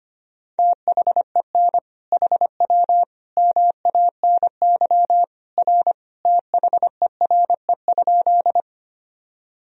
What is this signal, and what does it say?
Morse code, heard as a keyed tone, about 25 wpm, 700 Hz. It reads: THEN HW MANY R THERE?
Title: Sentences from Top 100 Words and Abbreviations 25wpm